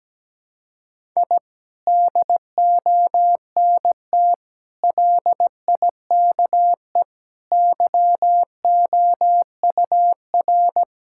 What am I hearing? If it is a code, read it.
Morse code, heard as a keyed tone, about 17 wpm, 700 Hz. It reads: I DONT LIKE YOUR
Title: Sentences from Top 1000 Words - Review Part 6 - 17wpm